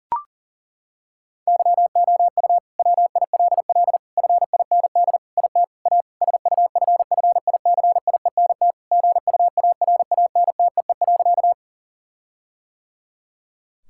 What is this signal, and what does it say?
Morse code, heard as a keyed tone, about 40 wpm, 700 Hz. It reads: YOU WILL FIND IT A SUFFICIENT GUARANTEE.
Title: Sentences from Top 1600 Words 40wpm